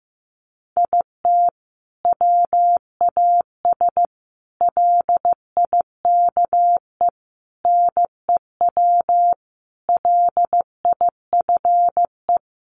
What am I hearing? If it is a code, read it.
Morse code, heard as a keyed tone, about 15 wpm, 700 Hz. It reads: IT WAS LIKE NEW LIFE
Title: Sentences from Top 300 Words 15wpm